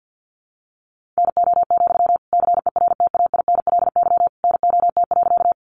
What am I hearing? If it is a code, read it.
Morse code, heard as a keyed tone, about 50 wpm, 700 Hz. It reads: NO, CERTAINLY NOT.
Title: Sentences from Top 900 Words 50wpm